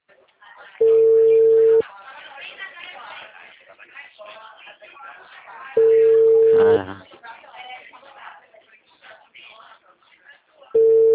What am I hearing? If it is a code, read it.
Ringback tone (China, 450 Hz), with speech.
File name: out- -2004